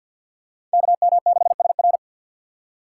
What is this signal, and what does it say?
Morse code, heard as a keyed tone, about 52 wpm, 700 Hz. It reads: KM6SR
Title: USA Call Signs - 5 Characters - 6x Character Spacing - 50wpm